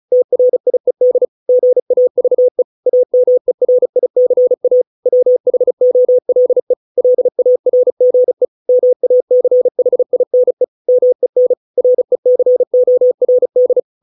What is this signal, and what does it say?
Morse code, heard as a keyed tone, about 34 wpm, 500 Hz. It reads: TRIED GAVE AMERICA WHOLE LARGE MACHINE MEN RECORD